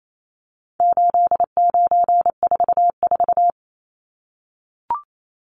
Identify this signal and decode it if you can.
Morse code, heard as a keyed tone, about 28 wpm, 700 Hz. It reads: 8944
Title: Numbers - 4-Digit 28wpm